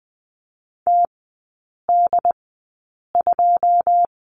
Morse code, heard as a keyed tone, about 20 wpm, spaced as for 8 wpm, 700 Hz. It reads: TD2
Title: 178 Lesson 29e - Slash - Three Characters 20wpm